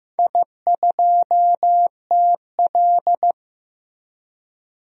Morse code, heard as a keyed tone, about 16 wpm, 700 Hz. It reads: I2TL
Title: DXCC-100 Call Signs - 4 Characters - Repeated 3x - 15wpm